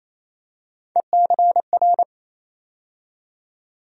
Morse code, heard as a keyed tone, about 28 wpm, 700 Hz. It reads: ECR
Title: USA Call Sign Postfixes - 3 Letters 28wpm